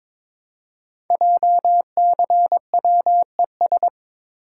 Morse code, heard as a keyed tone, about 22 wpm, 700 Hz. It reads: JCWES
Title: Code Groups - 5 Letters - 22wpm